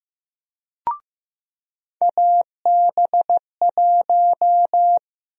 Morse code, heard as a keyed tone, about 15 wpm, 700 Hz. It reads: AB1
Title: USA Call Sign Prefixes - 15wpm